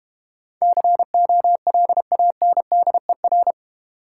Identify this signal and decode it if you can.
Morse code, heard as a keyed tone, about 30 wpm, 700 Hz. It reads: COLANDER
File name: Easy Words - Speed-Racing with Hints - 25wpm